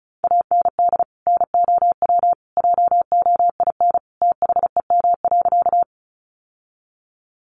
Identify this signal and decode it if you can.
Morse code, heard as a keyed tone, about 34 wpm, 700 Hz. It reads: AND NOW JOIN THEM.
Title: Sentences from Top 1400 Words 35wpm